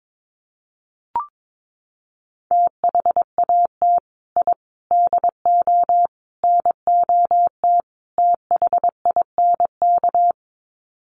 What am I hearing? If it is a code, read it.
Morse code, heard as a keyed tone, about 22 wpm, 700 Hz. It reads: THAT I DO NOT THINK
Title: Sets of 5 Words Encore 22wpm